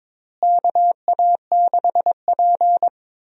Morse code, heard as a keyed tone, about 22 wpm, 700 Hz. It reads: KA6P
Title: USA Call Signs - 4 Characters - Repeated 2x - 22wpm